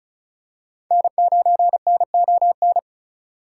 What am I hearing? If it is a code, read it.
Morse code, heard as a keyed tone, about 34 wpm, 690 Hz. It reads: N9NON
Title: USA Call Signs - 5 Characters - 2x Character Spacing - 35wpm